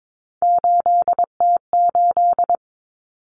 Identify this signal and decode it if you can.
Morse code, heard as a keyed tone, about 22 wpm, 700 Hz. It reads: 8T8
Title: Common Cut Numbers - 3-Digit 22wpm